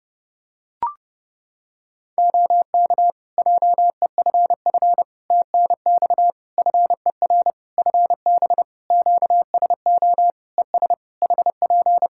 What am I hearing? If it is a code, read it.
Morse code, heard as a keyed tone, about 32 wpm, 700 Hz. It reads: OK JEFF TNX FER FB QSO ES HP
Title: K1USN SST QSO Practice 30wpm